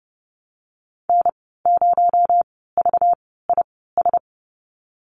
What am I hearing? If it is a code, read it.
Morse code, heard as a keyed tone, about 30 wpm, spaced as for 17 wpm, 700 Hz. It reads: N0VIS
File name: Lesson 37g - Letter x - Call Signs 30wpm